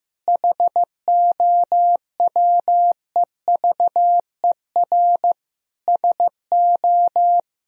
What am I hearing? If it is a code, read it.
Morse code, heard as a keyed tone, about 14 wpm, 700 Hz. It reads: HOWEVER SO
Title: Sentences from Top 1000 Words - Review Part 2 - 15wpm